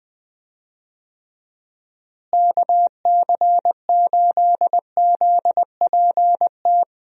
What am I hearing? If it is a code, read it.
Morse code, heard as a keyed tone, about 20 wpm, 700 Hz. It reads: KC8ZPT
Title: USA Call Signs - 6 Characters - Repeated 3x - 20wpm